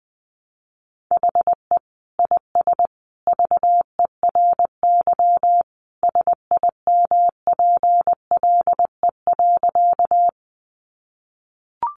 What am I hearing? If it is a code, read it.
Morse code, heard as a keyed tone, about 19 wpm, 700 Hz. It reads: HE IS VERY SIMPLE.
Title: Sentences from Top 400 Words 20wpm